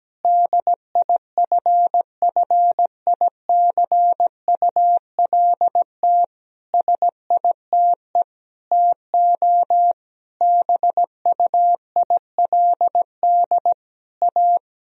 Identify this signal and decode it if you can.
Morse code, heard as a keyed tone, about 17 wpm, 700 Hz. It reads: DIFFICULT SITE TO BUILD A
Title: Sentences from Top 1000 Words - Review Part 4 - 17wpm